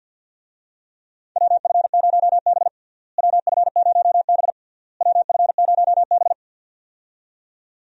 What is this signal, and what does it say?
Morse code, heard as a keyed tone, about 53 wpm, 700 Hz. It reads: WF0B WF0B WF0B
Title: USA Call Signs - 4 Characters - Repeated 3x - 50wpm